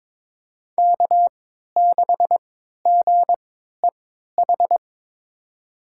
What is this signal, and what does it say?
Morse code, heard as a keyed tone, about 21 wpm, spaced as for 12 wpm, 700 Hz. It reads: K6GEH
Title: USA Call Signs - 3x Character Spacing - 22wpm